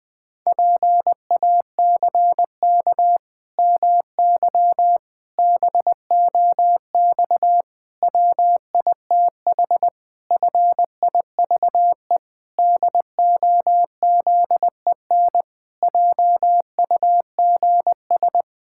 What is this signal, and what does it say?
Morse code, heard as a keyed tone, about 21 wpm, 700 Hz. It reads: PACK MY BOX WITH FIVE DOZEN JUGS
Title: Pangram Challenge v2 - 20wpm